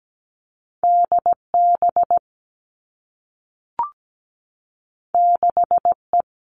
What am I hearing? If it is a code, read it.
Morse code, heard as a keyed tone, about 17 wpm, 700 Hz. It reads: D B 6 E
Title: Two Challenging Letter-Number Combinations 17wpm